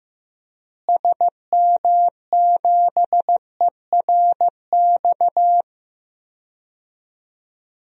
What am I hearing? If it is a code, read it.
Morse code, heard as a keyed tone, about 15 wpm, 700 Hz. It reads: SM7ERX
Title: DXCC-100 Call Signs - 6 Characters - Repeated 3x - 15wpm